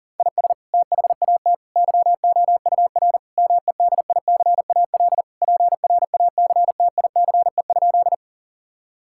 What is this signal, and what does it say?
Morse code, heard as a keyed tone, about 40 wpm, 700 Hz. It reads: IS THAT YOUR MEDICAL PRACTICE?
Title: Sentences from Top 1000 Words - Review Part 5 - 40wpm